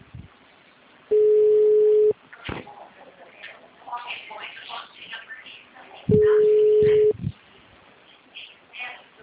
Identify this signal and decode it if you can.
Ringback tone (Europe, 425 Hz), with speech.